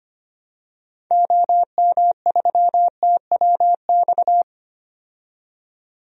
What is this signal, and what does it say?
Morse code, heard as a keyed tone, about 25 wpm, 700 Hz. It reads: OM3TWX
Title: DXCC-100 Call Signs - 6 Characters - 25wpm